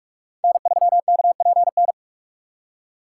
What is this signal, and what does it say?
Morse code, heard as a keyed tone, about 42 wpm, 700 Hz. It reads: N3KPN
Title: USA Call Signs - 6x Character Spacing - 45wpm